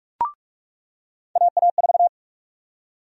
Morse code, heard as a keyed tone, about 46 wpm, 700 Hz. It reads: AA4
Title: USA Call Sign Prefixes - 45wpm